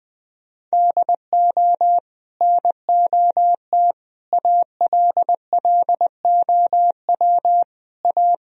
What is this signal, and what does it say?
Morse code, heard as a keyed tone, about 20 wpm, 700 Hz. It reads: DO NOT ALLOW A
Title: Sentences from Top 1000 Words - Review Part 4 - 20wpm